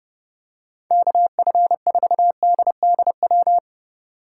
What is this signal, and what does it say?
Morse code, heard as a keyed tone, about 30 wpm, 700 Hz. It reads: KF4DDW
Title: USA Call Signs - 6 Characters - 3x Character Spacing - 30wpm